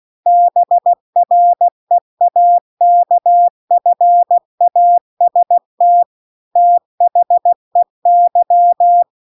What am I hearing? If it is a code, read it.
Morse code, heard as a keyed tone, about 16 wpm, 700 Hz. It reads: BREAKFAST THEY